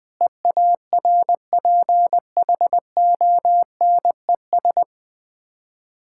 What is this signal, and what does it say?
Morse code, heard as a keyed tone, about 20 wpm, 690 Hz. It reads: EARPHONES
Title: Difficult Words with Hints - 20wpm